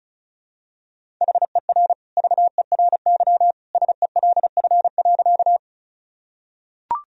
Morse code, heard as a keyed tone, about 36 wpm, 700 Hz. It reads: HER VERY SELF.
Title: Sentences from Top 1000 Words - Review Part 1 - 35wpm